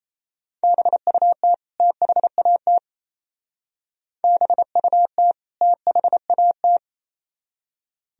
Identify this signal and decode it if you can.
Morse code, heard as a keyed tone, about 37 wpm, 700 Hz. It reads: BUT THAT BUT THAT